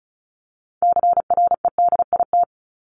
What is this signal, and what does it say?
Morse code, heard as a keyed tone, about 34 wpm, 700 Hz. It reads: CREDIT
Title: Top 751-1000 Words 35wpm